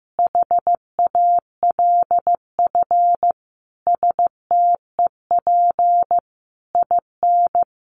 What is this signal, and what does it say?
Morse code, heard as a keyed tone, about 15 wpm, 700 Hz. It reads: HALF STEP IN
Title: Sentences from Top 1400 Words 15wpm